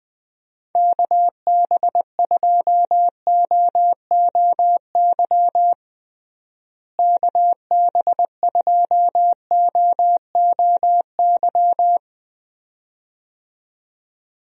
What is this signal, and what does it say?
Morse code, heard as a keyed tone, about 21 wpm, 700 Hz. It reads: KB2OOY KB2OOY
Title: USA Call Signs - 6 Characters - Repeated 2x - 20wpm